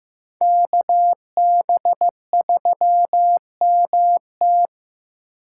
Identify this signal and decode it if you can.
Morse code, heard as a keyed tone, about 15 wpm, 700 Hz. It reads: KB3MT
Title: USA Call Signs - 6x Character Spacing - 15wpm